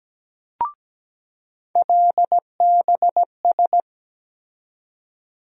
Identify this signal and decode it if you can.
Morse code, heard as a keyed tone, about 17 wpm, 700 Hz. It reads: LBS
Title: USA Call Sign Postfixes - 3 Letters 17wpm